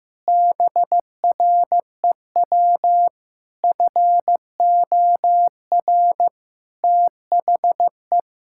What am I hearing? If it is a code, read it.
Morse code, heard as a keyed tone, about 15 wpm, 700 Hz. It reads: BREW FOR THE
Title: Pangram Challenge v2 - 15wpm